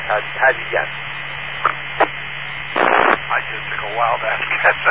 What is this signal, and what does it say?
Emergency Alert System (SAME) tones, heard as harsh buzzing bursts, with speech. It is heard from 0.9 to 3.8 s.